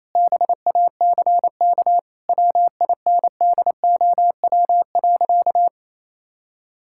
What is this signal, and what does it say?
Morse code, heard as a keyed tone, about 28 wpm, 700 Hz. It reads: BACK WINDOW.
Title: Sentences from Top 1600 Words 28wpm